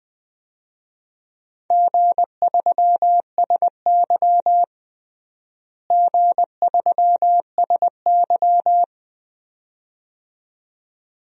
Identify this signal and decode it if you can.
Morse code, heard as a keyed tone, about 20 wpm, 700 Hz. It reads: G3SY G3SY
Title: DXCC-100 Call Signs - 4 Characters - Repeated 2x - 20wpm